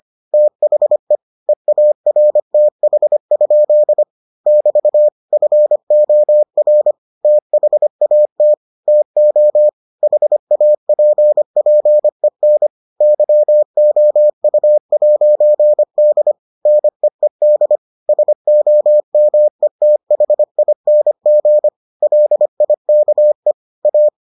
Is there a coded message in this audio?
Yes: Morse code from 0.3 to 24.1 s, heard as a keyed tone, about 25 wpm, 600 Hz. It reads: THE EARTH? <BT> FOR THAT TO HAPPEN YOU'D NEED SOMETHING LIKE A